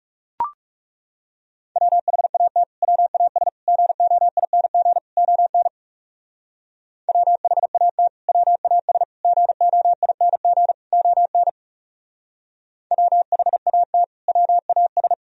Morse code, heard as a keyed tone, about 35 wpm, 700 Hz. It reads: WHAT WAS GOING ON WHAT WAS GOING ON WHAT WAS
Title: Speed-Racing - Sets of 4 Words 30wpm